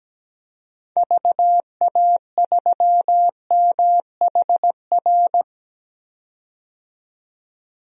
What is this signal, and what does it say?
Morse code, heard as a keyed tone, about 17 wpm, 700 Hz. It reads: VA3MHR
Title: DXCC-100 Call Signs - 6 Characters - 17wpm